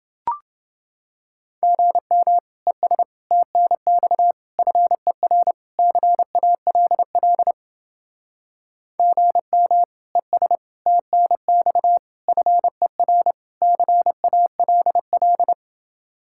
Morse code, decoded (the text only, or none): GM ES TNX FER CALL GM ES TNX FER CALL